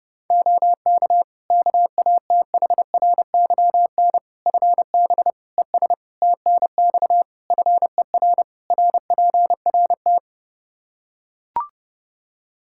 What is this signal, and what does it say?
Morse code, heard as a keyed tone, about 30 wpm, 700 Hz. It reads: OK KATHRYN FB ES TNX FER RPRT
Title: K1USN SST QSO Practice 30wpm